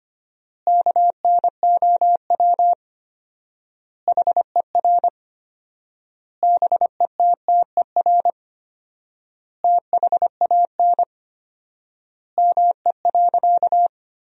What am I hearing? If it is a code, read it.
Morse code, heard as a keyed tone, about 25 wpm, 700 Hz. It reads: KNOW HER BETTER THAN ME.
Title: 4x Word-Spacing - Sentences from Top 400 Words - 25wpm